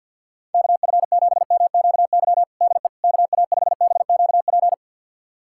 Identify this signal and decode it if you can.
Morse code, heard as a keyed tone, about 51 wpm, 700 Hz. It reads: KF7MQY DE KA5BQP
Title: USA Callsign DE Callsign - 6 Characters - 50wpm